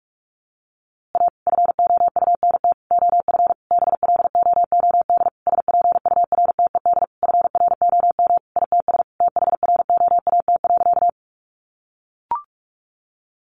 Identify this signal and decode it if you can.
Morse code, heard as a keyed tone, about 44 wpm, 700 Hz. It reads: A FOUNT OF BLOOD SPURTED FROM ITS THROAT.
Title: A Sound Of Thunder 45wpm